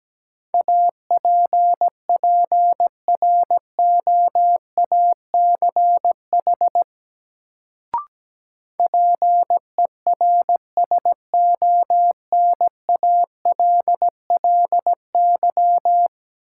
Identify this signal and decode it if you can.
Morse code, heard as a keyed tone, about 17 wpm, 700 Hz. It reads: APPROACH PERSONALLY
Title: Sentences from Top 1400 Words - New Word Drill Alt - 17wpm